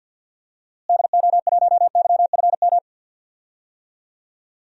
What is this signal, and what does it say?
Morse code, heard as a keyed tone, about 46 wpm, 700 Hz. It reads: DO1YFM
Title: DXCC-100 Call Signs - 6 Characters - Repeated 2x - 50wpm